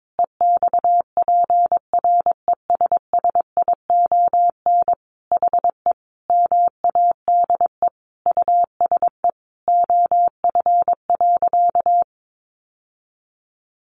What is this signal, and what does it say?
Morse code, heard as a keyed tone, about 22 wpm, 700 Hz. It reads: EXPRESSION HE MADE USE OF.
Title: Sentences from Top 1200 Words 22wpm